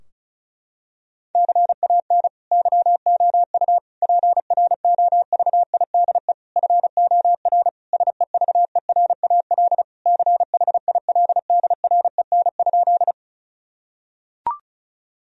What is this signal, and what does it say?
Morse code, heard as a keyed tone, about 36 wpm, 700 Hz. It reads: CAN YOU PROVIDE FOR SEVERAL CHILDREN?
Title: Sentences from Top 1000 Words - Review Part 2 - 35wpm